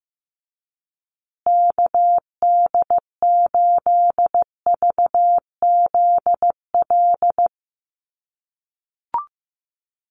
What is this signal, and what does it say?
Morse code, heard as a keyed tone, about 15 wpm, 700 Hz. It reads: KD8VZL
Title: USA Call Signs Repeated 3x 15wpm